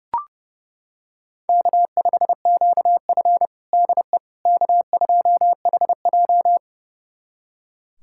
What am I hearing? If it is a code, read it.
Morse code, heard as a keyed tone, about 30 wpm, 700 Hz. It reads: K5QF DE K2HJ
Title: USA Callsign DE Callsign - 4 Characters - 30wpm